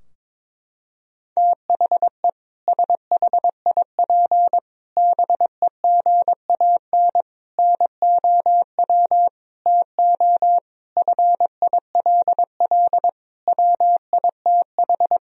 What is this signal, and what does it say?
Morse code, heard as a keyed tone, about 21 wpm, 700 Hz. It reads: THE SHIP BEGAN NOW TO FILL WITH
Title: Sentences from Top 1000 Words - Review Part 3 - 22wpm